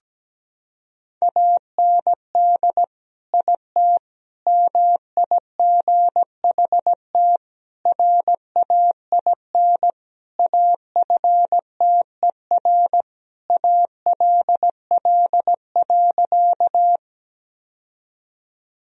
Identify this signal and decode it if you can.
Morse code, heard as a keyed tone, about 17 wpm, 700 Hz. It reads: AND IT MIGHT RAIN AFTER ALL.
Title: Sentences from Top 1000 Words - Review Part 1 - 17wpm